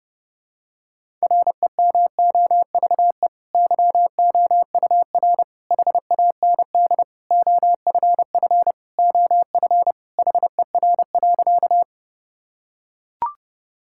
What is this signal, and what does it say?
Morse code, heard as a keyed tone, about 30 wpm, 700 Hz. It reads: REMOVE YOUR HAND OFF OF HER.
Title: Sentences from Top 1000 Words - Review Part 5 - 30wpm